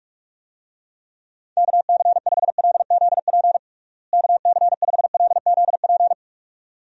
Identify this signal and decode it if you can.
Morse code, heard as a keyed tone, about 41 wpm, 700 Hz. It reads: KC5LZP KC5LZP
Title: USA Call Signs - 6 Characters - Repeated 2x - 45wpm